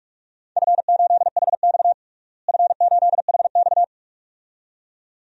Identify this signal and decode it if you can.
Morse code, heard as a keyed tone, about 46 wpm, 700 Hz. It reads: F8HX F8HX
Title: DXCC-100 Call Signs - 4 Characters - Repeated 2x - 45wpm